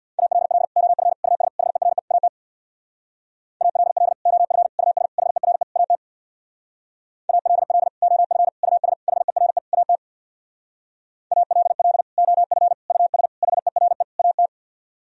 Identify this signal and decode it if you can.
Morse code, heard as a keyed tone, about 65 wpm, 700 Hz. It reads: ALL OF US HERE AT ALL OF US HERE AT ALL OF US HERE AT ALL OF US HERE AT